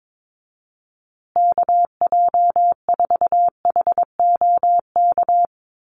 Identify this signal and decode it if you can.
Morse code, heard as a keyed tone, about 22 wpm, 700 Hz. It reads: KJ4HOK
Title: USA Call Signs Repeated 2x 22wpm